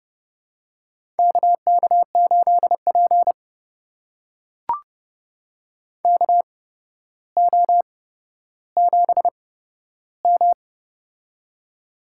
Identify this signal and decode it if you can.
Morse code, heard as a keyed tone, about 30 wpm, spaced as for 13 wpm, 700 Hz. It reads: KK8P KO7M
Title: USA Call Signs - 4 Characters - 8x Character Spacing - 30wpm